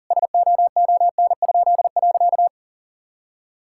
Morse code, heard as a keyed tone, about 37 wpm, 700 Hz. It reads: SOON?.
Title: Pangram Challenge v2 - 40wpm